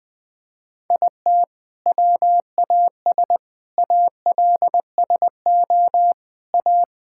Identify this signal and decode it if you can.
Morse code, heard as a keyed tone, about 20 wpm, 700 Hz. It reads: IT WAS ALSO A
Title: Sentences from Top 1600 Words 20wpm